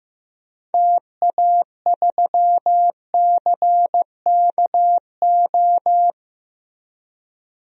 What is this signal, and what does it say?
Morse code, heard as a keyed tone, about 15 wpm, 700 Hz. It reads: TA3CKO
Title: DXCC-100 Call Signs - 6 Characters - 15wpm